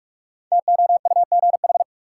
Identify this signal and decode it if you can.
Morse code, heard as a keyed tone, about 42 wpm, 700 Hz. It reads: TOUGH
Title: Sentences from Top 1200 Words - New Word Drill - 45wpm